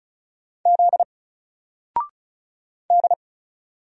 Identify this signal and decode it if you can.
Morse code, heard as a keyed tone, about 35 wpm, 700 Hz. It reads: Z D
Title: Single Letter-Number Alt 35wpm